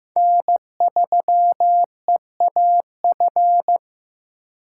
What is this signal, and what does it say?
Morse code, heard as a keyed tone, about 15 wpm, 700 Hz. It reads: N3EAF
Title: USA Call Signs - 5 Characters - 3x Character Spacing - 15wpm